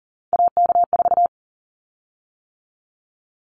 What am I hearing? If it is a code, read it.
Morse code, heard as a keyed tone, about 40 wpm, 700 Hz. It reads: AK4
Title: Callsign Prefixes for Top 25 DXCC - 40wpm